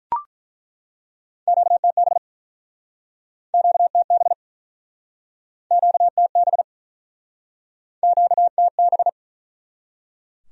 Morse code, heard as a keyed tone, about 49 wpm, 700 Hz. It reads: QTB QTB QTB QTB